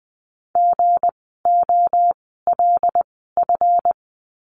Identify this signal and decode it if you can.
Morse code, heard as a keyed tone, about 19 wpm, spaced as for 14 wpm, 700 Hz. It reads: GOLF
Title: 160 Lesson 26f - Letter g - Words 20wpm